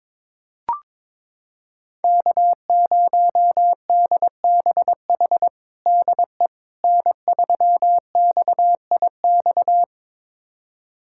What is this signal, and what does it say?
Morse code, heard as a keyed tone, about 22 wpm, 700 Hz. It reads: K0DBH DE N3XIX
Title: USA Callsign DE Callsign - 5 Characters - 22wpm